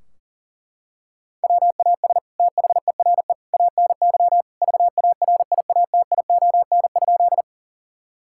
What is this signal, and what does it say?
Morse code, heard as a keyed tone, about 40 wpm, 700 Hz. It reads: WAS THERE ANY VARIATION?
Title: Sentences from Top 1600 Words 40wpm